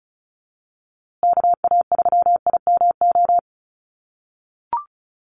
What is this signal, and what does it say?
Morse code, heard as a keyed tone, about 34 wpm, 700 Hz. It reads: KA3IMO
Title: USA Call Signs Repeated 3x 35wpm